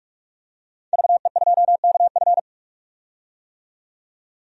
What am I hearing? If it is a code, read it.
Morse code, heard as a keyed tone, about 47 wpm, 700 Hz. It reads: VE2KF
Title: DXCC-100 Call Signs - 5 Characters - 45wpm